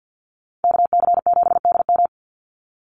Morse code, heard as a keyed tone, about 50 wpm, 700 Hz. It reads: KC7DM
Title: NA SOTA Activator Callsigns 50wpm